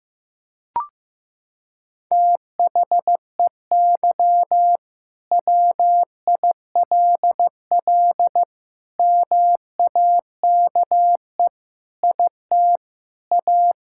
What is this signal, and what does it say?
Morse code, heard as a keyed tone, about 15 wpm, 700 Hz. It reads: THEY WILL MAKE IT A
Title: Sentences from Top 1600 Words 15wpm